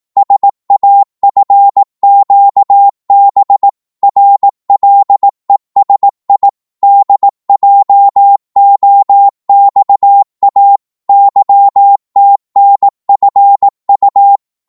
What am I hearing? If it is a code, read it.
Morse code, heard as a keyed tone, about 18 wpm, 800 Hz. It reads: SAFQB RLESI DJOXA YTNFU
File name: Morse / ltr-18wpm